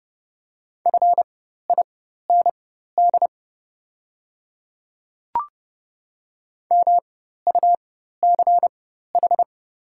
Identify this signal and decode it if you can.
Morse code, heard as a keyed tone, about 31 wpm, spaced as for 14 wpm, 700 Hz. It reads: FIND MUCH
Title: 4x Character Spacing - Top 200 Words - 30wpm